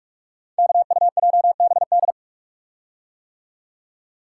Morse code, heard as a keyed tone, about 46 wpm, 700 Hz. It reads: KUJBD
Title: Code Groups - 5 Letter:Numbers - 45wpm